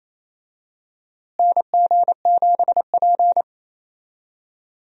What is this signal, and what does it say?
Morse code, heard as a keyed tone, about 28 wpm, 700 Hz. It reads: NG7P
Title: USA Call Signs - 4 Characters - Repeated 2x - 28wpm